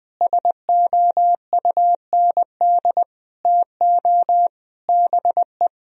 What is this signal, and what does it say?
Morse code, heard as a keyed tone, about 20 wpm, 700 Hz. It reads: SOUND TO BE
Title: Sentences from Top 1000 Words - Review Part 1 - 20wpm